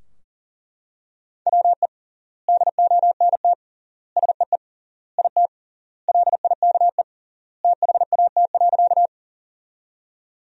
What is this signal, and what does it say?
Morse code, heard as a keyed tone, about 40 wpm, 700 Hz. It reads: WE DONT SEE IT LIKE THAT.
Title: 3x Word-Spacing - Sentences from Top 300 Words - 40wpm